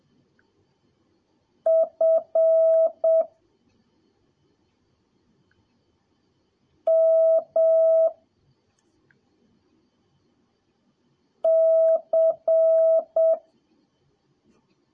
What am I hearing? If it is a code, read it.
Morse code, heard as a keyed tone, about 7 wpm, 650 Hz. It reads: F M C
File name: jh8cbh / 2021cw / cw03a